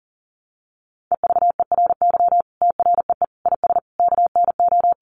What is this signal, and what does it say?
Morse code, heard as a keyed tone, about 40 wpm, 700 Hz. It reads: EVERY TREE IS KNO
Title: Sentences from Top 1300 Words 40wpm